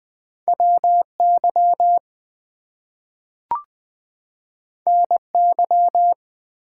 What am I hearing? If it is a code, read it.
Morse code, heard as a keyed tone, about 20 wpm, 700 Hz. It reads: WY NY
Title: US States and Canadian Provinces 20wpm